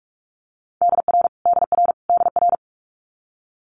Morse code, heard as a keyed tone, about 46 wpm, 700 Hz. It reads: DR DR DR
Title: Multi-letter Phonemes Repeated 3x 45wpm